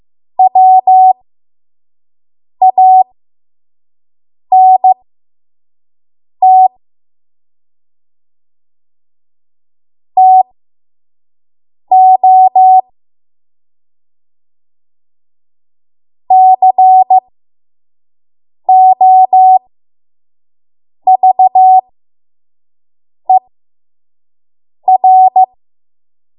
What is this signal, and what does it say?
Morse code, heard as a keyed tone, about 15 wpm, spaced as for 5 wpm, 750 Hz. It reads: WANT TO COVER
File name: Morse / 05WPM